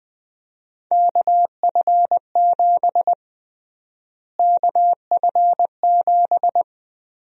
Morse code, heard as a keyed tone, about 21 wpm, 700 Hz. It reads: KF7 KF7
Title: USA Call Sign Prefixes - Repeated 2x - 20wpm